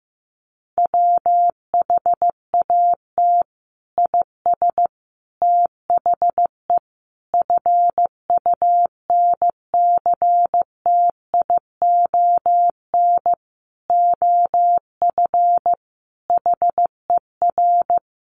Morse code, heard as a keyed tone, about 15 wpm, 700 Hz. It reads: WHAT IS THE FUNCTION OF HER
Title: Sentences from Top 1100 Words 15wpm